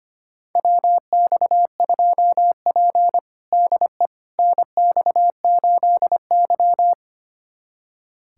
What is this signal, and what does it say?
Morse code, heard as a keyed tone, about 25 wpm, 700 Hz. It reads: WX2P DE NX8Y
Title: USA Callsign DE Callsign - 4 Characters - 25wpm